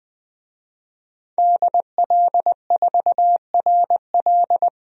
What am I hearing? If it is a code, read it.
Morse code, heard as a keyed tone, about 21 wpm, 700 Hz. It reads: DL4RL
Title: DXCC-100 Call Signs - 5 Characters - 20wpm